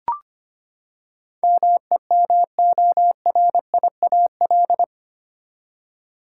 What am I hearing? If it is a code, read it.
Morse code, heard as a keyed tone, about 25 wpm, 700 Hz. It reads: MEMORIAL
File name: Eight Letter Words - 25wpm